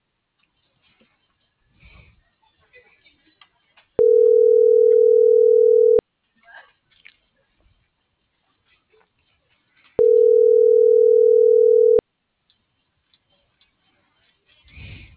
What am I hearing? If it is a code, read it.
Ringback tone (North America), 440 + 480 Hz.